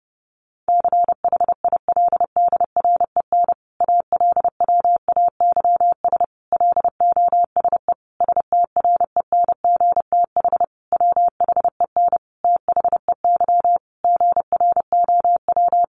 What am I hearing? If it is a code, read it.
Morse code, heard as a keyed tone, about 30 wpm, 700 Hz. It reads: CHILDREN ALWAYS LOSE STRENGTH WHEN THEY GROW
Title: Sentences from Top 1300 Words 30wpm